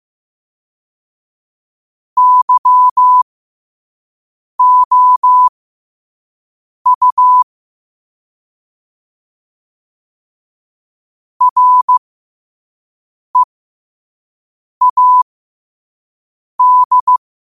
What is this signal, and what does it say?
Morse code, heard as a keyed tone, about 15 wpm, spaced as for 5 wpm, 1000 Hz. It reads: YOU READ
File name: qotdfiles / 5wpm